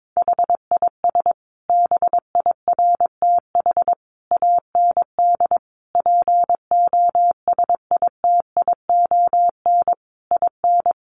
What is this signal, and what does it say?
Morse code, heard as a keyed tone, about 22 wpm, 700 Hz. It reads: HIS BIRTH AND POSITION IN
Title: Sentences from Top 900 Words 22wpm